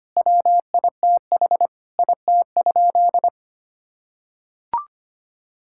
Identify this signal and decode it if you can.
Morse code, heard as a keyed tone, about 25 wpm, 700 Hz. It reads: WITH IT?
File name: practice / 25wpm / Sentences with 4-Word Phrases - 25wpm